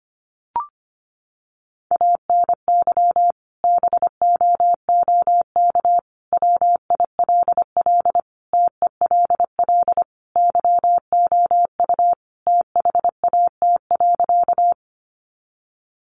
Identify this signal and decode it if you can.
Morse code, heard as a keyed tone, about 25 wpm, 700 Hz. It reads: ANY BOOK WILL TELL YOU THAT.
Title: Sentences from Top 300 Words 25wpm